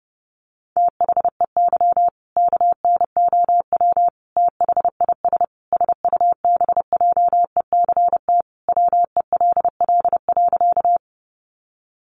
Morse code, heard as a keyed tone, about 30 wpm, 700 Hz. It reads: THEY KNOW THIS SUBJECT WELL.
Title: Sentences from Top 800 Words 30wpm